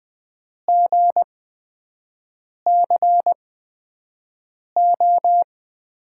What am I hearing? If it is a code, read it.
Morse code, heard as a keyed tone, about 20 wpm, spaced as for 5 wpm, 700 Hz. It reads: GCO
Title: USA Call Signs - 6 Characters - 8x Character Spacing - 20wpm